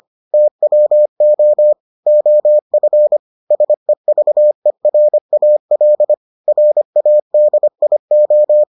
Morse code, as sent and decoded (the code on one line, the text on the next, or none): - .-- --- / --- ..-. / ... ...- . .-. .- .-.. / .-. .- -.. .. ---
TWO OF SEVERAL RADIO